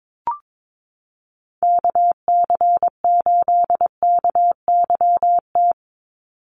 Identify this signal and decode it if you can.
Morse code, heard as a keyed tone, about 22 wpm, 700 Hz. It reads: KC8KYT
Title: USA Call Signs 22wpm